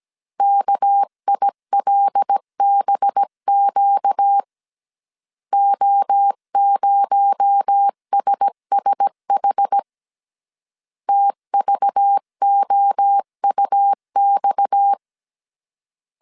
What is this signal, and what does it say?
Morse code, heard as a keyed tone, about 17 wpm, 800 Hz. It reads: KILBQ O0SSH TVOUX
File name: CW / DET / 1 6 1500C S C 16wpm 2733